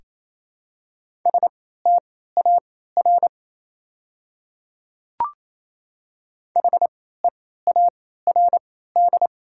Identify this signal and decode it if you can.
Morse code, heard as a keyed tone, about 27 wpm, spaced as for 16 wpm, 700 Hz. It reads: STAR HEARD
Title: 3x Character Spacing - Top 500 Words - 28wpm